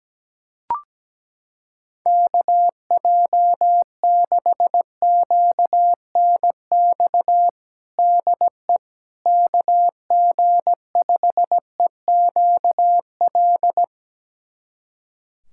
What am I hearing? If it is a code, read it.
Morse code, heard as a keyed tone, about 17 wpm, 700 Hz. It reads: KJ6QNX DE KG5EQL
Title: USA Callsign DE Callsign - 6 Characters - 17wpm